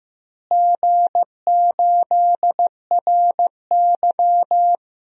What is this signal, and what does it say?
Morse code, heard as a keyed tone, about 15 wpm, 700 Hz. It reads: G8RY
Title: DXCC-100 Call Signs - 4 Characters - Repeated 2x - 15wpm